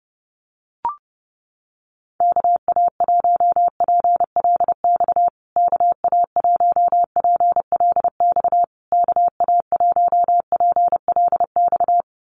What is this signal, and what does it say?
Morse code, heard as a keyed tone, about 30 wpm, 700 Hz. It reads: KA1PLX KA1PLX KA1PLX
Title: USA Call Signs Repeated 3x 30wpm